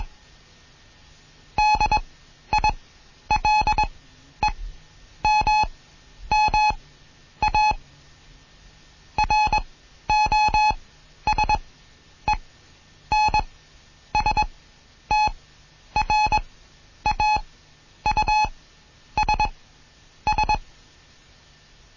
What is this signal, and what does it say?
Morse code, heard as a keyed tone, about 21 wpm, spaced as for 10 wpm, 850 Hz. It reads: DILEMMA ROSENSTRAUSS
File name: Morsen / Kurs A 20